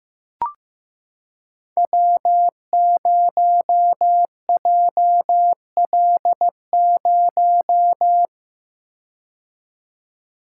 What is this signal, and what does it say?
Morse code, heard as a keyed tone, about 15 wpm, 700 Hz. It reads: W0JL0
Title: Code Groups - 5 Letter:Numbers - 15wpm